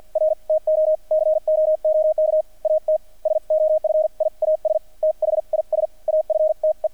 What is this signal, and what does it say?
Morse code, heard as a keyed tone, about 50 wpm, spaced as for 43 wpm, 620 Hz, with radio noise. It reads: R TO COOK AT SOFIAS THIS AFTE
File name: archived / CRSnet